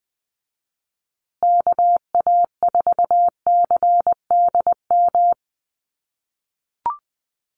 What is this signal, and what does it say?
Morse code, heard as a keyed tone, about 20 wpm, 700 Hz. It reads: KA4CDM
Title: USA Call Signs Repeated 2x 20wpm